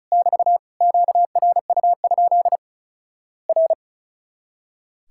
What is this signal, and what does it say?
Morse code, heard as a keyed tone, about 36 wpm, 700 Hz. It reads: <BT> QRU? R
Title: Worked All Europe DX Contest - QTC Exchange - 35wpm